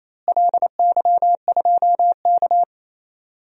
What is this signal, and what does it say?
Morse code, heard as a keyed tone, about 28 wpm, 700 Hz. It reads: LY2K
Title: DXCC-100 Call Signs - 4 Characters - Repeated 3x - 28wpm